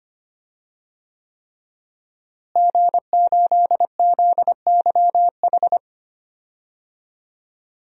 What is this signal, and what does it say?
Morse code, heard as a keyed tone, about 25 wpm, 700 Hz. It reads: G8ZYH
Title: DXCC-100 Call Signs - 5 Characters - Repeated 3x - 25wpm